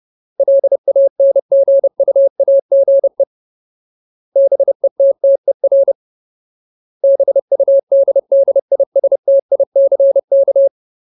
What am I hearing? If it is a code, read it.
Morse code, heard as a keyed tone, about 30 wpm, 550 Hz. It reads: LANGUAGE BETTER BUDDISTICK
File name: Ultimate Word List / 30WPM Ultimate